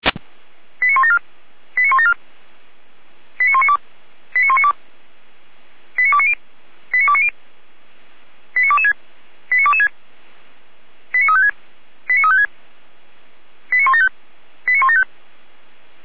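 Five-tone selective call (ZVEI), heard as quick runs of five tones, tones of about 70 ms. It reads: 89174 89174 89192 89192 89290 89290 89226 89226 89356 89356 89175 89175
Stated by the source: ZVEI five-tone selective call received by SDR.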